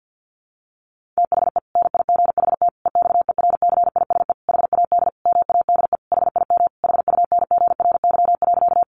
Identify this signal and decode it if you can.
Morse code, heard as a keyed tone, about 50 wpm, 700 Hz. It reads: THE NIGHT EXERCISE HAD MADE HIM HUNGRY.
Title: Sentences from Top 1400 Words 50wpm